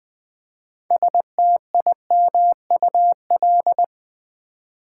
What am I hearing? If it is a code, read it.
Morse code, heard as a keyed tone, about 20 wpm, 700 Hz. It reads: STIMUL
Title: Build A Word - 3 Letters At A Time - 9-Letter Words - 20wpm